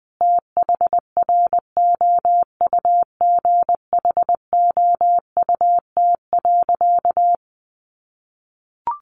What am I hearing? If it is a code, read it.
Morse code, heard as a keyed tone, about 20 wpm, 700 Hz. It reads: THROUGHOUT.
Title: Sentences from Top 800 Words 20wpm